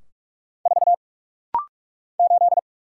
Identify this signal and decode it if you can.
Morse code, heard as a keyed tone, about 46 wpm, 700 Hz. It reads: 4 8
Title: Single Letter-Number - Rapid-Fire 45wpm